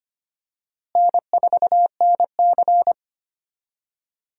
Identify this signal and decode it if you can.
Morse code, heard as a keyed tone, about 25 wpm, 700 Hz. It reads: N4NC
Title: USA Call Signs - 4 Characters - Repeated 3x - 25wpm